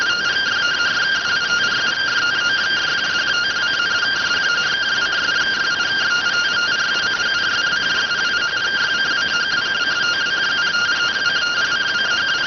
Two-tone teleprinter signal, about 170 Hz shift.